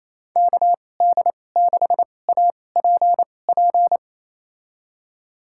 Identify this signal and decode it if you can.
Morse code, heard as a keyed tone, about 28 wpm, spaced as for 20 wpm, 700 Hz. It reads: KD6APP
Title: USA Call Signs - 6 Characters - 2x Character Spacing - 28wpm